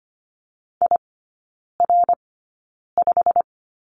Morse code, heard as a keyed tone, about 25 wpm, spaced as for 9 wpm, 700 Hz. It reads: IR5
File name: Lesson 17e - Letter u - Three Characters 25wpm